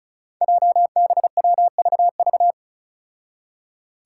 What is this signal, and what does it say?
Morse code, heard as a keyed tone, about 36 wpm, 700 Hz. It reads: JBWVV
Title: Five Challenging Letter-Number Combinations 35wpm